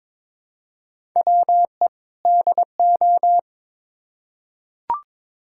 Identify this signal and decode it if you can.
Morse code, heard as a keyed tone, about 22 wpm, 700 Hz. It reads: WE DO
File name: Build A Phrase - 2-Word Phrases - 22wpm